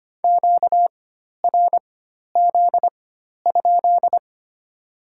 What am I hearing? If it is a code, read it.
Morse code, heard as a keyed tone, about 25 wpm, spaced as for 12 wpm, 700 Hz. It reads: QRZ?
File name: Top 90 QSO Elements - 4x Character Spacing - 25wpm